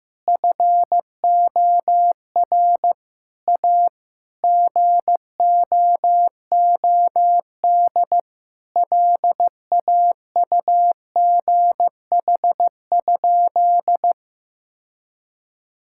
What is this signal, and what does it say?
Morse code, heard as a keyed tone, about 15 wpm, 700 Hz. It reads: FOR A GOOD LAUGH?
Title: Sentences from Top 1000 Words - Review Part 5 - 15wpm